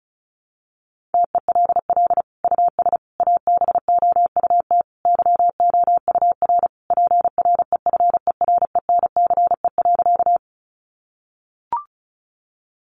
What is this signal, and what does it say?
Morse code, heard as a keyed tone, about 34 wpm, 700 Hz. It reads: TELL US ABOUT YOUR PREFERENCE.
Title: Sentences from Top 1500 Words 35wpm